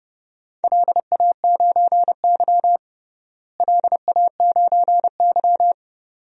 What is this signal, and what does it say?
Morse code, heard as a keyed tone, about 30 wpm, 700 Hz. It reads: LA9Y LA9Y
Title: DXCC-100 Call Signs - 4 Characters - Repeated 2x - 30wpm